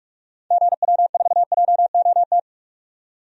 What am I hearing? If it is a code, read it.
Morse code, heard as a keyed tone, about 47 wpm, 700 Hz. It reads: GW4JOT
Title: DXCC-100 Call Signs - 6 Characters - 45wpm